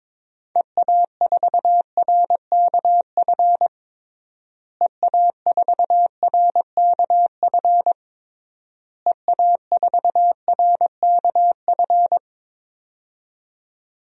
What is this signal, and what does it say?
Morse code, heard as a keyed tone, about 23 wpm, 700 Hz. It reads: EA4RKF EA4RKF EA4RKF
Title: DXCC-100 Call Signs - 6 Characters - Repeated 3x - 22wpm